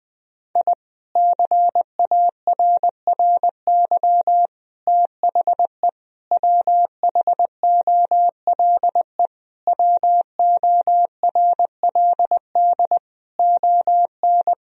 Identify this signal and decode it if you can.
Morse code, heard as a keyed tone, about 21 wpm, 700 Hz. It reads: I CARRY THE WHOLE WORLD ON
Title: Sentences from Top 1000 Words - Review Part 1 - 20wpm